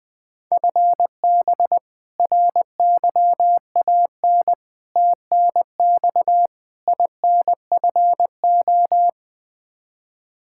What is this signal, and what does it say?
Morse code, heard as a keyed tone, about 20 wpm, 700 Hz. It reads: FB RYAN TNX INFO
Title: K1USN SST QSO Practice 20wpm